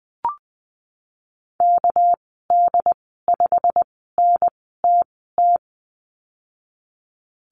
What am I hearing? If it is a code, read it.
Morse code, heard as a keyed tone, about 20 wpm, spaced as for 15 wpm, 700 Hz. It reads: KD5NTT
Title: 200 Lesson 32g - Letter k - Call Signs 20wpm